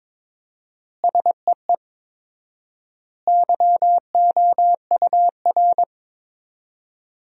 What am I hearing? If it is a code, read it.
Morse code, heard as a keyed tone, about 21 wpm, 700 Hz. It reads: SEE YOUR
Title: 4x Word-Spacing - Sentences from Top 500 Words - 22wpm